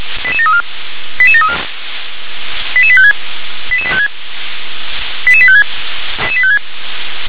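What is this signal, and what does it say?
Five-tone selective call (ZVEI), heard as quick runs of five tones, tones of about 70 ms. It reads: 88734 88734 88856 88856 88856
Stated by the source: Received on SDR.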